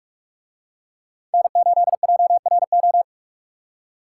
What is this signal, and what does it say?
Morse code, heard as a keyed tone, about 41 wpm, 700 Hz. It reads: N8JRO